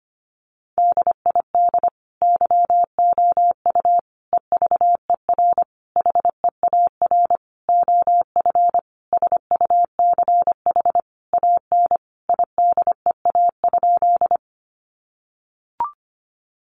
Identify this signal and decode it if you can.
Morse code, heard as a keyed tone, about 25 wpm, 700 Hz. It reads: DID YOU EVER HEAR OF SUCH AN IDEA?
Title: Sentences from Top 400 Words 25wpm